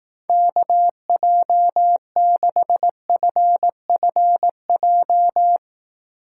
Morse code, heard as a keyed tone, about 18 wpm, 700 Hz. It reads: KJ6FFJ
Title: Speed Racing - USA 2x3 Call Signs 15wpm